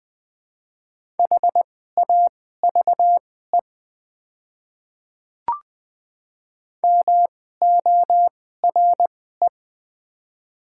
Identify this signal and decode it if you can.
Morse code, heard as a keyed tone, about 20 wpm, spaced as for 14 wpm, 700 Hz. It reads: HAVE MORE
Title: 2x Character Spacing - Top 300 Words - 20wpm